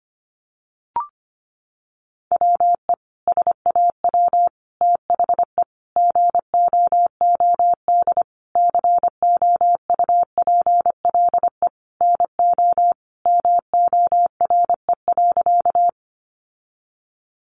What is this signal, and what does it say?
Morse code, heard as a keyed tone, about 25 wpm, 700 Hz. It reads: WE SAW THE GOOD COUPLE NO MORE.
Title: Sentences from Top 1100 Words 25wpm